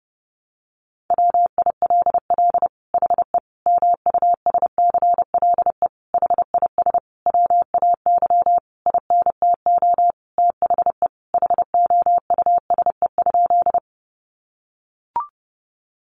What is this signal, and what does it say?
Morse code, heard as a keyed tone, about 30 wpm, 700 Hz. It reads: WILL HE MUSCLE HIS WAY INTO THE HOUSE?
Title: Sentences from Top 1300 Words 30wpm